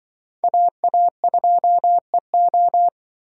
Morse code, heard as a keyed tone, about 24 wpm, 700 Hz. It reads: AA2EO
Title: Speed Racing - USA Call Signs 20wpm